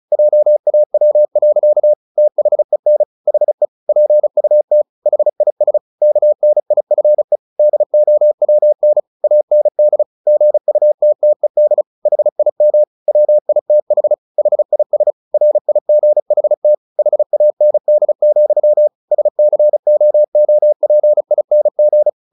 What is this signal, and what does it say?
Morse code, heard as a keyed tone, about 34 wpm, 600 Hz. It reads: JAW. THEN HE PUT HIS KNIFE DOWN AND GUTTED HIM WITH HIS RIGHT HAND, SCOOPING
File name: the old man and the sea 2 35wpm